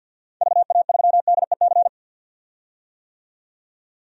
Morse code, heard as a keyed tone, about 50 wpm, 700 Hz. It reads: VA3DEC
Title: DXCC-100 Call Signs - 6 Characters - 50wpm